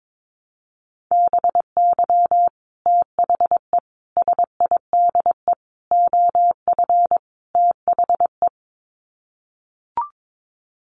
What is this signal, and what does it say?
Morse code, heard as a keyed tone, about 22 wpm, 700 Hz. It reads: BY THE SIDE OF THE
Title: Sets of 5 Words Encore2 22wpm